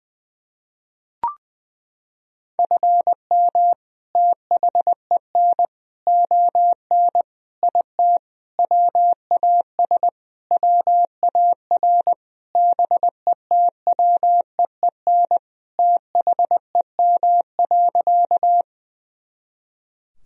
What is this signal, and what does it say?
Morse code, heard as a keyed tone, about 20 wpm, 700 Hz. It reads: FM THEN ON IT WAS WAR BETWEEN THEM.
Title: Sentences from Top 400 Words and Abbreviations 20wpm